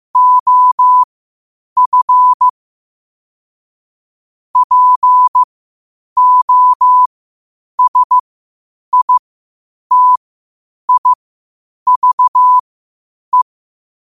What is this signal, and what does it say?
Morse code, heard as a keyed tone, about 15 wpm, spaced as for 8 wpm, 1000 Hz. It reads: OF POSITIVE
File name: qotdfiles / 8wpm